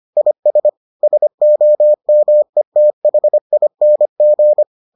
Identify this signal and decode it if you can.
Morse code, heard as a keyed tone, about 25 wpm, 600 Hz. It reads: IS SOMETHING